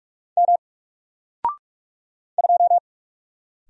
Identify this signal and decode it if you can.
Morse code, heard as a keyed tone, about 46 wpm, 700 Hz. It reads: M 2
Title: Single Letter-Number Alt 45wpm